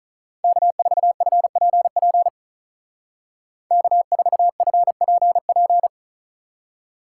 Morse code, heard as a keyed tone, about 47 wpm, 700 Hz. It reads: K4FPP K4FPP